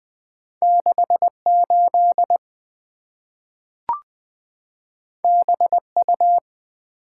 Morse code, heard as a keyed tone, about 20 wpm, 690 Hz. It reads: 68 BU
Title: Two Letter-Number Combinations - 20wpm